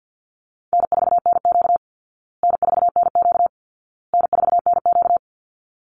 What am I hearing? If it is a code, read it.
Morse code, heard as a keyed tone, about 50 wpm, 700 Hz. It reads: N4NQ N4NQ N4NQ
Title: Call Signs Repeated 2x - 50wpm